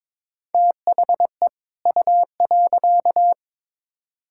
Morse code, heard as a keyed tone, about 22 wpm, 700 Hz. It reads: THE U.
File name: Build A Phrase - 2-Word Phrases - 22wpm